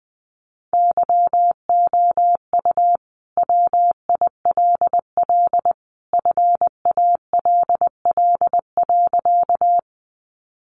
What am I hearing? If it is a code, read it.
Morse code, heard as a keyed tone, about 19 wpm, 700 Hz. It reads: YOU WILL FALL.
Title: Sentences from Top 500 Words 20wpm